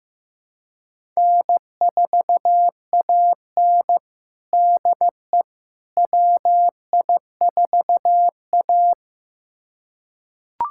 Morse code, heard as a keyed tone, about 16 wpm, 700 Hz. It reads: N4AN DE WI4A
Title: USA Callsign DE Callsign - 4 Characters - 15wpm